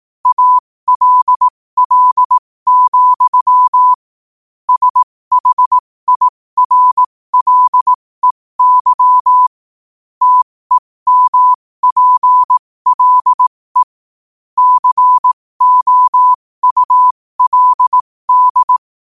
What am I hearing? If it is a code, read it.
Morse code, heard as a keyed tone, about 18 wpm, spaced as for 15 wpm, 1000 Hz. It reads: ALL, SHIRLEY TEMPLE COULD
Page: New quotes every day in morse code at 15 Words per minute.